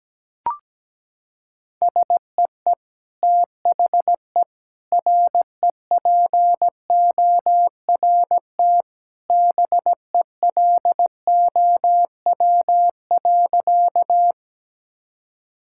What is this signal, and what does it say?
Morse code, heard as a keyed tone, about 17 wpm, 700 Hz. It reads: SEE THE REPORT BELOW.
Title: Sentences from Top 1000 Words - Review Part 6 - 17wpm